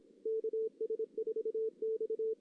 Morse code, heard as a keyed tone, about 25 wpm, 440 Hz. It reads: KS4X
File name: 2016 JAN CW NAQP / 2016 Jan naqp k5ot - 0-3